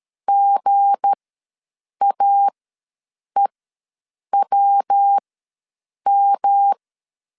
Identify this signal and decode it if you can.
Morse code, heard as a keyed tone, about 13 wpm, spaced as for 7 wpm, 800 Hz. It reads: GAEWM
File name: LEZIONI / 9 286 ws 12wpm 3351